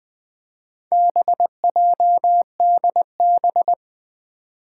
Morse code, heard as a keyed tone, about 20 wpm, 700 Hz. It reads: BJDB
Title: Four Challenging Letter-Number Combinations 20wpm